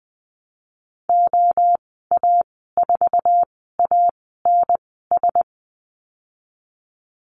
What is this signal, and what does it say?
Morse code, heard as a keyed tone, about 20 wpm, spaced as for 15 wpm, 700 Hz. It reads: OA4ANS
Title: 055 Lesson 10g - Letter r - Call Signs 20wpm